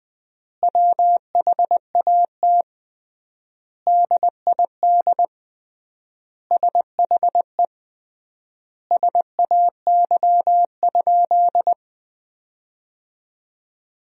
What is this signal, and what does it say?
Morse code, heard as a keyed tone, about 21 wpm, 700 Hz. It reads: WHAT DID SHE SAY?
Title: 3x Word-Spacing - Sentences from Top 500 Words - 20wpm